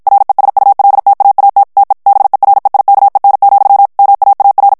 Morse code, heard as a keyed tone, about 55 wpm, 800 Hz. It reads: RESULTAAT TE BEREIKEN, MAAR
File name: HOLLANDAIS texte clair 2 vitesse 50 mots minute